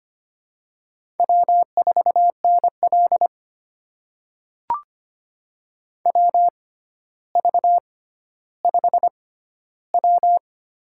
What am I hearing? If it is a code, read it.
Morse code, heard as a keyed tone, about 25 wpm, spaced as for 13 wpm, 700 Hz. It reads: W4NL WV5W
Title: USA Call Signs - 4 Characters - 6x Character Spacing - 25wpm